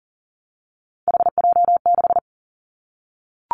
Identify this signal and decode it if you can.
Morse code, heard as a keyed tone, about 42 wpm, 690 Hz. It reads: HJ6
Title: Callsign Prefixes for Top 100 DXCC - 40wpm